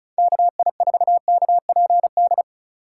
Morse code, heard as a keyed tone, about 34 wpm, 690 Hz. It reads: KI4KPD
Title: USA Call Signs - 6 Characters - 2x Character Spacing - 35wpm